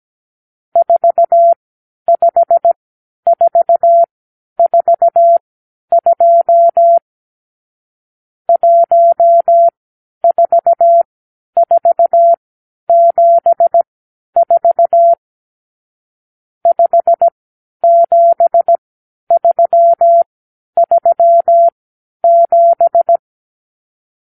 Morse code, heard as a keyed tone, about 17 wpm, spaced as for 10 wpm, 680 Hz. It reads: 45442 14474 57337
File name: k7qo / Lesson 52 (8)